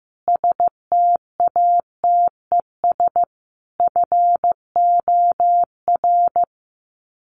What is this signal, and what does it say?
Morse code, heard as a keyed tone, about 15 wpm, 700 Hz. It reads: STATES FOR
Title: Sets of 5 Words Encore2 15wpm